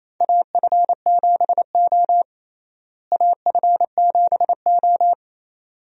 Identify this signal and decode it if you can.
Morse code, heard as a keyed tone, about 28 wpm, 700 Hz. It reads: AF7O AF7O
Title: USA Call Signs - 4 Characters - Repeated 2x - 28wpm